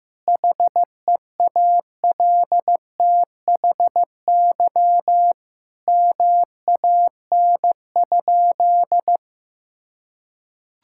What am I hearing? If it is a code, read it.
Morse code, heard as a keyed tone, about 15 wpm, 700 Hz. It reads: HEALTHY MAN?
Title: Sentences from Top 1000 Words - Review Part 5 - 15wpm